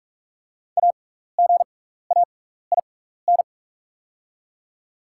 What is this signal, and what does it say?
Morse code, heard as a keyed tone, about 44 wpm, spaced as for 15 wpm, 690 Hz. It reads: AGAIN
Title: 6x Character Spacing - Top 200 Words - 45wpm